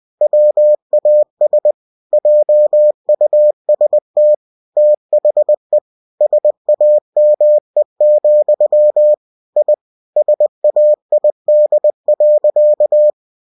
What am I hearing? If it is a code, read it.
Morse code, heard as a keyed tone, about 20 wpm, 600 Hz. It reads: WAS JUST THE SAME, I SAID.
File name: cw-books / winnie-the-pooh / winnie-20wpm-page-0111